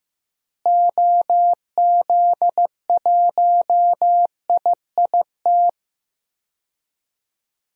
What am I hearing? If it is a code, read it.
Morse code, heard as a keyed tone, about 15 wpm, 700 Hz. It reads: OZ1IIT
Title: DXCC-100 Call Signs - 6 Characters - Repeated 3x - 15wpm